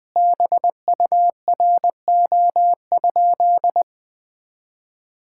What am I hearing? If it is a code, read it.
Morse code, heard as a keyed tone, about 20 wpm, 700 Hz. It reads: BURO?
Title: Common QSO Phrases - 4x Word-Spacing - 20wpm